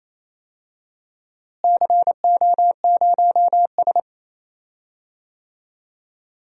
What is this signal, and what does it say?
Morse code, heard as a keyed tone, about 27 wpm, 700 Hz. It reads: CO0S
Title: DXCC-100 Call Signs - 4 Characters - Repeated 3x - 28wpm